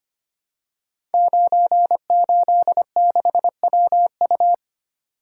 Morse code, heard as a keyed tone, about 25 wpm, 700 Hz. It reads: 986WU
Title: Code Groups - 5 Letter:Numbers - 25wpm